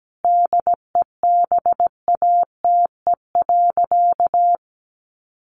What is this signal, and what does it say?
Morse code, heard as a keyed tone, about 17 wpm, 700 Hz. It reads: DEBATE.
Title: Sentences from Top 1100 Words 17wpm